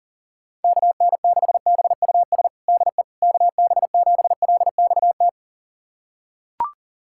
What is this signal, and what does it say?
Morse code, heard as a keyed tone, about 40 wpm, 690 Hz. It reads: KN6BUS DE KB7LXT
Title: USA Callsign DE Callsign - 6 Characters - 40wpm